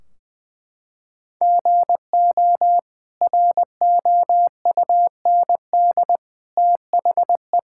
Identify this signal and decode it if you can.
Morse code, heard as a keyed tone, about 20 wpm, 700 Hz. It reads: GO ROUND THE
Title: Sentences from Top 1600 Words 20wpm